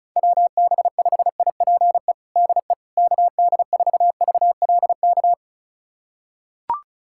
Morse code, heard as a keyed tone, about 36 wpm, 700 Hz. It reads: WB5IPE DE KD4VLK
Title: USA Callsign DE Callsign 35wpm